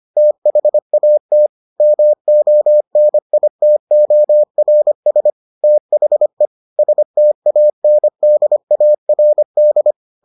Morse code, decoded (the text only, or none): THAT MONITORS THE STANDARD